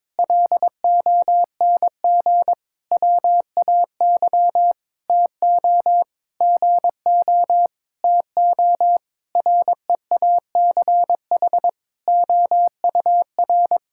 Morse code, heard as a keyed tone, about 22 wpm, 700 Hz. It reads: LONG WAY TO GO TO REACH OUR
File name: Sentences with 7-Word Phrases - 22wpm